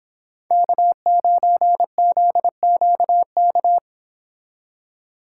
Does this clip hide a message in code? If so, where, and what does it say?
Yes: Morse code from 0.5 to 3.8 s, heard as a keyed tone, about 25 wpm, 700 Hz. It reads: K9ZQK